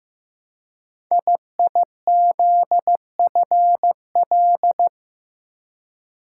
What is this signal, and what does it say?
Morse code, heard as a keyed tone, about 15 wpm, 700 Hz. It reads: IIZFL
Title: Code Groups - 5 Letters - 15wpm